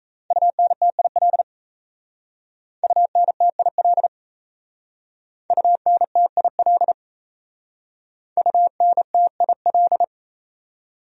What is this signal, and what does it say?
Morse code, heard as a keyed tone, about 34 wpm, 700 Hz. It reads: UNTIL UNTIL UNTIL UNTIL
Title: Speed-Racing Top 300 Words 28wpm